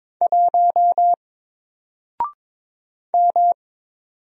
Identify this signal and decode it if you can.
Morse code, heard as a keyed tone, about 22 wpm, 700 Hz. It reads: 1 M
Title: Single Character (Letter:Number:Punctuation:Prosign) 22wpm